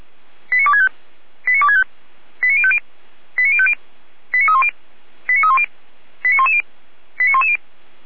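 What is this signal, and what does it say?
Five-tone selective call (ZVEI), heard as quick runs of five tones, tones of about 70 ms. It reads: 89265 89265 89050 89050 89310 89310 89110 89110
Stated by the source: ZVEI five-tone call 89078 received by SDR.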